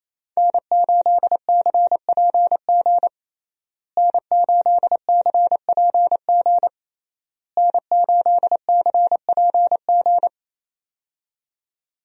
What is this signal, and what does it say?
Morse code, heard as a keyed tone, about 27 wpm, 700 Hz. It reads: N8CPG N8CPG N8CPG
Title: USA Call Signs - 5 Characters - Repeated 3x - 28wpm